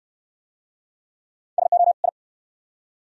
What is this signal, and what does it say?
Morse code, heard as a keyed tone, about 69 wpm, 690 Hz. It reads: SO I